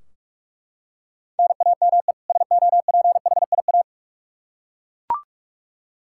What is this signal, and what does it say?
Morse code, heard as a keyed tone, about 41 wpm, 700 Hz. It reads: NAME SOPHIA
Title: Practice QSO - Exchange 1 - Beginning - 45wpm